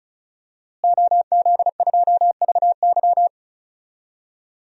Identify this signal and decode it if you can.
Morse code, heard as a keyed tone, about 36 wpm, 700 Hz. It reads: OZ2VY
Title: DXCC-100 Call Signs - 5 Characters - Repeated 3x - 35wpm